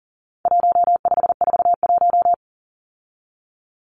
Morse code, heard as a keyed tone, about 40 wpm, 700 Hz. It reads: 1541
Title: Numbers - 4-Digit 40wpm